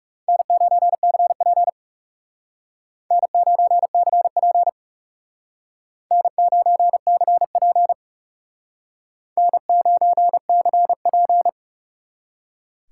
Morse code, heard as a keyed tone, about 38 wpm, 700 Hz. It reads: N9CP N9CP N9CP N9CP